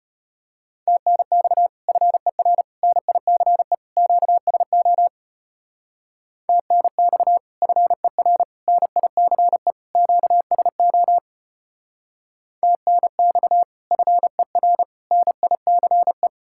Morse code, heard as a keyed tone, about 30 wpm, 700 Hz. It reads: TNX FER NICE QSO TNX FER NICE QSO TNX FER NICE
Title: Speed-Racing Common QSO Phrases Encore - 25wpm